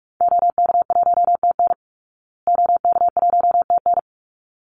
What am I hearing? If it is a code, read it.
Morse code, heard as a keyed tone, about 44 wpm, 700 Hz. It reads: OK1TN OK1TN
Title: Call Signs Repeated 3x - 45wpm